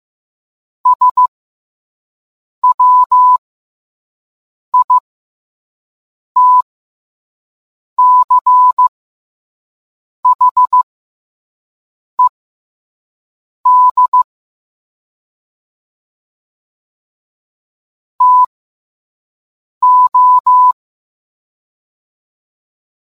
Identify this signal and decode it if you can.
Morse code, heard as a keyed tone, about 15 wpm, spaced as for 5 wpm, 1000 Hz. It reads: SWITCHED TO